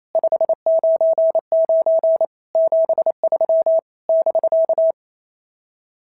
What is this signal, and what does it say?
Morse code, heard as a keyed tone, about 28 wpm, 650 Hz. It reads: 599 73 <BK>
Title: SOTA QSO Practice 28wpm